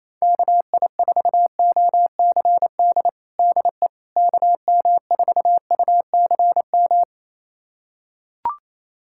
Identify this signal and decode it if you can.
Morse code, heard as a keyed tone, about 28 wpm, 700 Hz. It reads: KI4OCD DE KM4UCM
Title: USA Callsign DE Callsign - 6 Characters - 28wpm